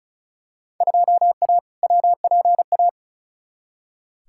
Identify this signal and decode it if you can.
Morse code, heard as a keyed tone, about 34 wpm, 700 Hz. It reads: 2A WPA
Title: Field Day Exchanges - 35wpm